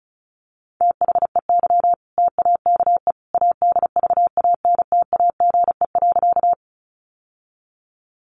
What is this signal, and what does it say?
Morse code, heard as a keyed tone, about 34 wpm, 700 Hz. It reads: THEY TAKE ADVANTAGE.
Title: Sentences from Top 900 Words 35wpm